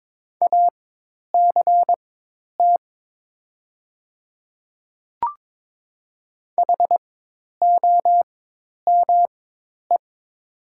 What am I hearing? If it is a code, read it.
Morse code, heard as a keyed tone, about 22 wpm, spaced as for 10 wpm, 700 Hz. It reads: ACT HOME
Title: 4x Character Spacing - Top 300 Words - 22wpm